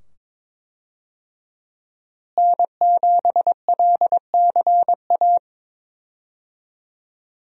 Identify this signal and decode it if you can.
Morse code, heard as a keyed tone, about 21 wpm, 700 Hz. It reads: N7LCA
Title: USA Call Signs - 5 Characters - Repeated 2x - 22wpm